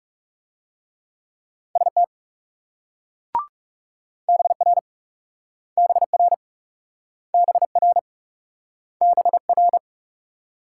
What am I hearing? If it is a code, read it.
Morse code, heard as a keyed tone, about 39 wpm, 700 Hz. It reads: ST BR BR BR BR